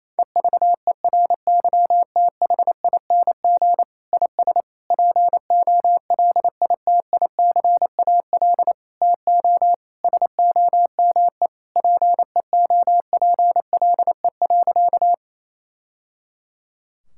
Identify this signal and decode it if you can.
Morse code, heard as a keyed tone, about 28 wpm, 700 Hz. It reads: EVERYTHING IS POLITICAL TO SOME PEOPLE.
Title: Sentences from Top 1000 Words - Review Part 5 - 28wpm